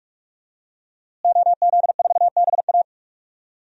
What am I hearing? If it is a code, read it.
Morse code, heard as a keyed tone, about 45 wpm, 700 Hz. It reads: OZ4BA
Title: DXCC-100 Call Signs - 5 Characters - Repeated 3x - 45wpm